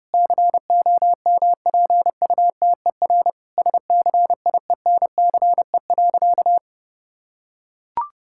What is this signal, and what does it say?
Morse code, heard as a keyed tone, about 32 wpm, 700 Hz. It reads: COMPUTER SCIENCE.
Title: Sentences from Top 1000 Words - Review Part 6 - 30wpm